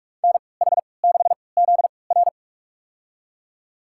Morse code, heard as a keyed tone, about 42 wpm, spaced as for 25 wpm, 700 Hz. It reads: NH6ZR
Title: USA Call Signs - 5 Characters - 3x Character Spacing - 45wpm